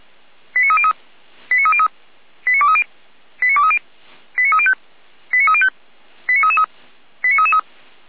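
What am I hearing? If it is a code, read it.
Five-tone selective call (ZVEI), heard as quick runs of five tones, tones of about 80 ms. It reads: 89303 89303 89230 89230 89305 89305 89333 89333
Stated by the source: ZVEI five-tone selective call 89315 received by SDR, archived by Feuerwehr Orlamünde.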